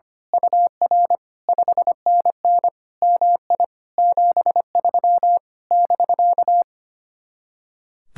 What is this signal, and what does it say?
Morse code, heard as a keyed tone, about 24 wpm, 700 Hz. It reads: UR 5NN MI 73 <BK>
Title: Speed-Racing Common QSO Phrases Encore - 25wpm